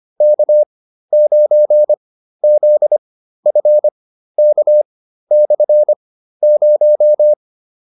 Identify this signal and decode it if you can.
Morse code, heard as a keyed tone, about 25 wpm, spaced as for 13 wpm, 600 Hz. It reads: K9ZFK/0
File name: QSO303 13